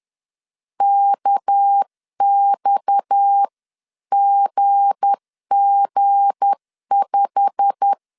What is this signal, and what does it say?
Morse code, heard as a keyed tone, about 10 wpm, 800 Hz. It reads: KXGG5
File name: LEZIONI / 9 28 10wpm 2842